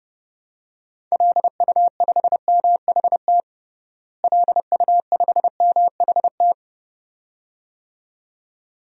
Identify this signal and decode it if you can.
Morse code, heard as a keyed tone, about 32 wpm, 700 Hz. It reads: LU5MHT LU5MHT
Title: DXCC-100 Call Signs - 6 Characters - Repeated 2x - 30wpm